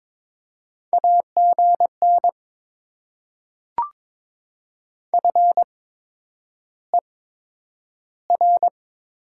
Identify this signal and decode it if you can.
Morse code, heard as a keyed tone, about 22 wpm, spaced as for 9 wpm, 700 Hz. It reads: AGN FER
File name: Top 20 QSO Elements - 8x Character Spacing - 22wpm